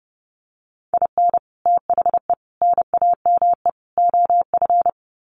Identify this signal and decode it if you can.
Morse code, heard as a keyed tone, about 30 wpm, 700 Hz. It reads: IN THE NAME OF
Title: Sets of 4 Words 30wpm